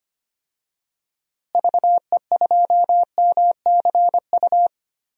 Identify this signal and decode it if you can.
Morse code, heard as a keyed tone, about 25 wpm, 700 Hz. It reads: VE2MCU
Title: DXCC-100 Call Signs - 6 Characters - Repeated 3x - 25wpm